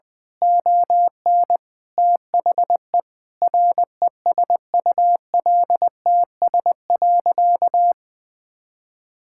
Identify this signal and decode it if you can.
Morse code, heard as a keyed tone, about 20 wpm, 700 Hz. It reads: ON THE RESULTS.
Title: Sentences with 7-Word Phrases - 20wpm